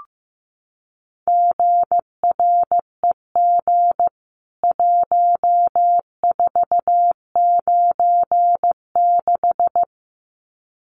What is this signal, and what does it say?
Morse code, heard as a keyed tone, about 15 wpm, 700 Hz. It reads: GREG 1496
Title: CWT Exchange 15wpm